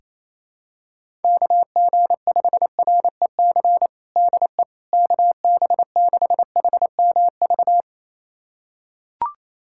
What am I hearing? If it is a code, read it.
Morse code, heard as a keyed tone, about 28 wpm, 700 Hz. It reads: KG5REC DE KB6HMV
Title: USA Callsign DE Callsign - 6 Characters - 28wpm